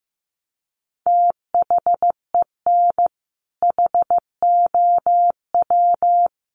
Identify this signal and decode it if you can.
Morse code, heard as a keyed tone, about 15 wpm, 700 Hz. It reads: THEN HOW
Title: Sentences from Top 100 Words 15wpm